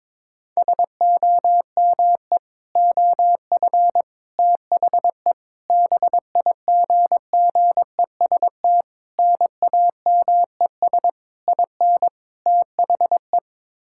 Morse code, heard as a keyed tone, about 22 wpm, 700 Hz. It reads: SOME OF THE BIGGEST NAMES IN THE
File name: practice / 22wpm / Sets of 7 Words - 22wpm